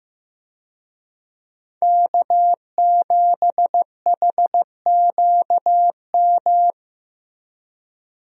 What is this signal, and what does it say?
Morse code, heard as a keyed tone, about 15 wpm, 700 Hz. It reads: K7HQM
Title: USA Call Signs - 5 Characters - Repeated 2x - 15wpm